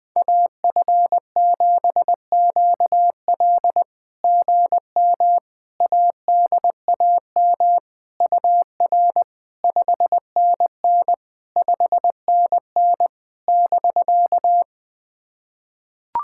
Morse code, heard as a keyed tone, about 20 wpm, 700 Hz. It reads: AF7QL GM ADAM UR 5NN 5NN <BK>
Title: SOTA QSO Practice 20wpm